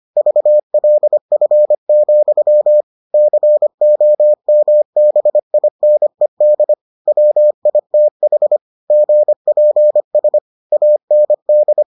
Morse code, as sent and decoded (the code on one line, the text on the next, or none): ...- .-.. ..-. --..-- / -.-. --- -- -... .. -. -.. / .-- .. - .... / --. .--. ... / .- -. -..
VLF, COMBINED WITH GPS AND